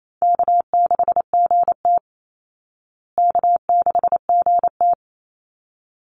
Morse code, heard as a keyed tone, about 28 wpm, 700 Hz. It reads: K6GT K6GT
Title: Easy Call Signs Repeated 2x - 28wpm